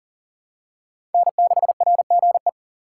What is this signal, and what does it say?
Morse code, heard as a keyed tone, about 40 wpm, 700 Hz. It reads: N6RGE
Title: USA Call Signs - 5 Characters - 2x Character Spacing - 40wpm